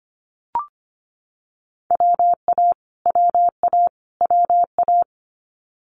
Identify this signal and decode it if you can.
Morse code, heard as a keyed tone, about 25 wpm, 700 Hz. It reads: WA WA WA
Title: Multi-letter Phonemes Repeated 3x 25wpm